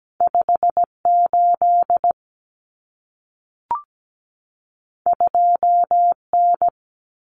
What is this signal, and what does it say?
Morse code, heard as a keyed tone, about 17 wpm, 700 Hz. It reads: 58 2N
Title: Common Cut Numbers - 2-Digit 17wpm